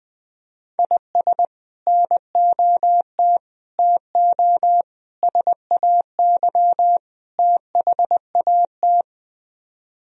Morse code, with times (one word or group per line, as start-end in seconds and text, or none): 0.8-1.4 IS
1.9-3.4 NOT
3.8-4.8 TO
5.2-7.0 SAY
7.4-9.0 THAT